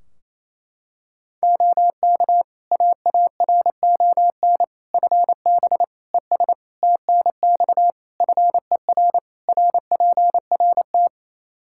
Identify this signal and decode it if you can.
Morse code, heard as a keyed tone, about 28 wpm, 700 Hz. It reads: OK AARON FB ES TNX FER RPRT
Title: Practice QSO - Exchange 2 - Info - 28wpm